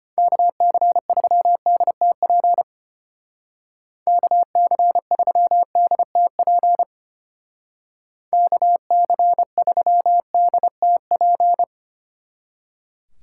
Morse code, heard as a keyed tone, about 32 wpm, 700 Hz. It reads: KC3DTP KC3DTP KC3DTP